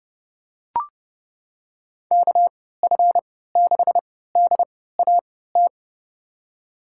Morse code, heard as a keyed tone, about 30 wpm, spaced as for 17 wpm, 700 Hz. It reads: KF6DAT
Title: USA Call Signs - 3x Character Spacing - 30wpm